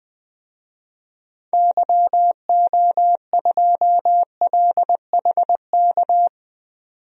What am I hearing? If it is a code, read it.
Morse code, heard as a keyed tone, about 20 wpm, 700 Hz. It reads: YO2LHK
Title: DXCC-100 Call Signs - 6 Characters - 20wpm